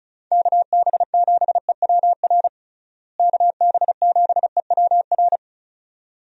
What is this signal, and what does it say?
Morse code, heard as a keyed tone, about 36 wpm, 700 Hz. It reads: KB7EWR KB7EWR
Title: USA Call Signs - 6 Characters - Repeated 2x - 35wpm